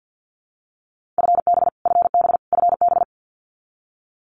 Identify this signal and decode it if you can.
Morse code, heard as a keyed tone, about 50 wpm, 700 Hz. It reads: FB FB FB
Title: Abbreviations Repeated 3x - 50wpm